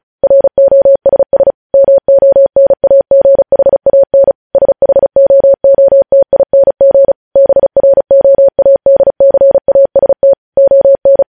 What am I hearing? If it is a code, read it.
Morse code, heard as a keyed tone, about 36 wpm, 550 Hz. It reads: ROSS MONAGHAN SHOOTING BROADCAST ON
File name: podcast / News headlines-1x-35x35